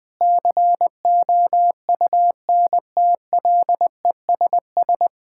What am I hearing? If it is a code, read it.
Morse code, heard as a keyed tone, about 21 wpm, 700 Hz. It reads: COUNTLESS
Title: Difficult Sentences with Hints - 20wpm